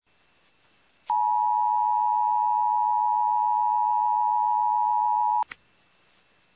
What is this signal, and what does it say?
Emergency Alert System (SAME) tones (the 853 and 960 Hz attention tone). It is heard from 1.1 to 5.5 s.
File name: eas / KUTTKGMT / KUTTKGMT